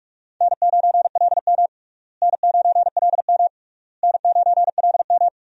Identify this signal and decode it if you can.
Morse code, heard as a keyed tone, about 46 wpm, 700 Hz. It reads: N9LM N9LM N9LM
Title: USA Call Signs - 4 Characters - Repeated 3x - 45wpm